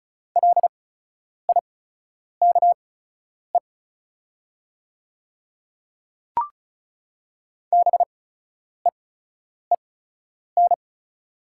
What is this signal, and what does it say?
Morse code, heard as a keyed tone, about 36 wpm, spaced as for 10 wpm, 690 Hz. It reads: LIKE BEEN
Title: 8x Character Spacing - Top 100 Words - 35wpm